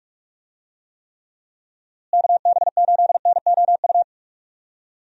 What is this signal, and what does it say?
Morse code, heard as a keyed tone, about 46 wpm, 700 Hz. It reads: KB8NOU
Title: USA Call Signs - 6 Characters - Repeated 2x - 45wpm